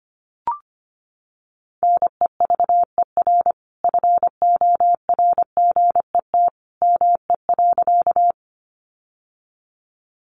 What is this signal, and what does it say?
Morse code, heard as a keyed tone, about 25 wpm, 700 Hz. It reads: NEVER FORGET ME.
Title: Sentences from Top 1200 Words 25wpm